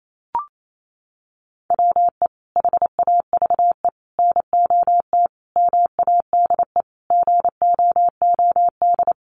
Morse code, heard as a keyed tone, about 28 wpm, 700 Hz. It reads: WE HAVE NOT MADE GOOD
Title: Sentences from Top 1400 Words 28wpm